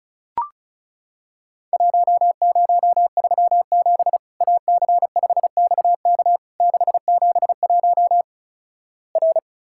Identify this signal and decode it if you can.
Morse code, heard as a keyed tone, about 36 wpm, 700 Hz. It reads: 1037 AC5XK 671 R
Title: Worked All Europe DX Contest - QTC Exchange - 35wpm